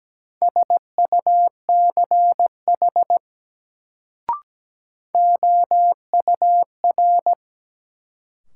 Morse code, heard as a keyed tone, about 17 wpm, 700 Hz. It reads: SUCH OUR
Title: Sentences from Top 200 Words - New Word Drill - 17wpm